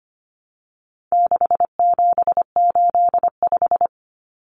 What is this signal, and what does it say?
Morse code, heard as a keyed tone, about 25 wpm, 700 Hz. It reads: 6785
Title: Common Cut Numbers - 4-Digit 25wpm